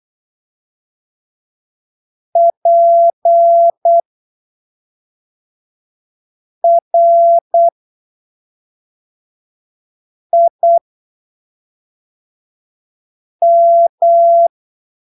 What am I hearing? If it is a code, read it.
Morse code, heard as a keyed tone, about 8 wpm, spaced as for 3 wpm, 680 Hz. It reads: PRIM
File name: arquivos / texto / texto 17 08ppm